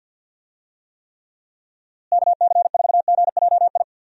Morse code, heard as a keyed tone, about 47 wpm, 700 Hz. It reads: KC4GJI
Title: USA Call Signs - 6 Characters - Repeated 2x - 50wpm